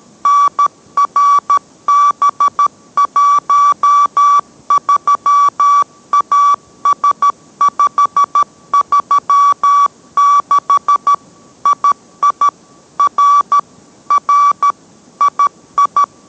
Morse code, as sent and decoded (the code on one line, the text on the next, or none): -. .-. -... .---- ...-- .- ... ..... ...-- -.... / .. .. / .-. / .-. / .. ..
NRB13AS536 II R R II